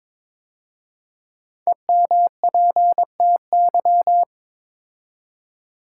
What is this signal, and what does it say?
Morse code, heard as a keyed tone, about 23 wpm, 700 Hz. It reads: EMPTY
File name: 5 Letter Words - First 3 Letters Given - 22wpm